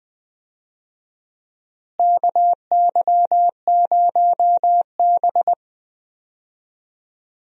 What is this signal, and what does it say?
Morse code, heard as a keyed tone, about 21 wpm, 700 Hz. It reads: KY0B
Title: USA Call Signs - 4 Characters - Repeated 3x - 20wpm